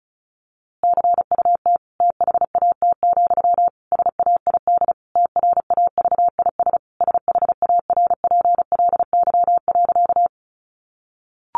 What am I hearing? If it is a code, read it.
Morse code, heard as a keyed tone, about 34 wpm, 700 Hz. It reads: CUT THAT, SAID TRAVIS SHARPLY.
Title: A Sound Of Thunder 35wpm